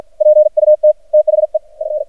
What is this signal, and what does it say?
Morse code, heard as a keyed tone, about 53 wpm, 610 Hz. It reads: PUT THE F
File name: archived / CRSnet